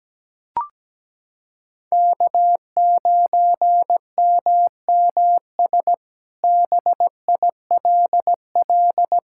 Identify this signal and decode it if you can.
Morse code, heard as a keyed tone, about 17 wpm, 700 Hz. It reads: K9MMS BILL
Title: K1USN SST QSO Practice - 17wpm